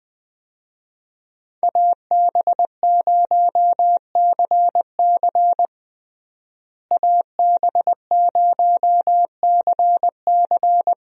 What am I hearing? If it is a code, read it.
Morse code, heard as a keyed tone, about 20 wpm, 700 Hz. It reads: AB0CC AB0CC
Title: USA Call Signs - 5 Characters - Repeated 2x - 20wpm